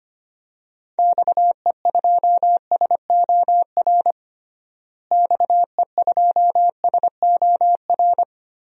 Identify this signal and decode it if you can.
Morse code, heard as a keyed tone, about 25 wpm, 700 Hz. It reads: XE2SOR XE2SOR
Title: DXCC-100 Call Signs - 6 Characters - Repeated 2x - 25wpm